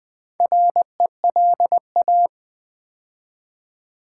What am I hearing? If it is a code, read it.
Morse code, heard as a keyed tone, about 20 wpm, 700 Hz. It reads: RELA
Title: Build A Word - 2 Letters At A Time - 8-Letter Words - 20wpm